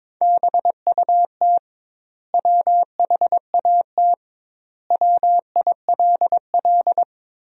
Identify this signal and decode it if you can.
Morse code, heard as a keyed tone, about 22 wpm, 700 Hz. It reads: BUT WHAT WILL
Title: 2x Word-Spacing - Sentences from Top 100 Words - 22wpm